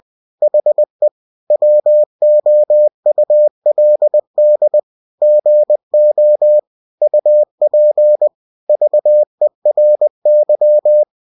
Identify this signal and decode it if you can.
Morse code, heard as a keyed tone, about 20 wpm, 600 Hz. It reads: HE WOULD GO UP VERY
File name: cw-books / winnie-the-pooh / winnie-20wpm-page-0041